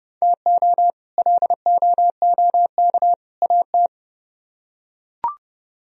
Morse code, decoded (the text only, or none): TO LOOK AT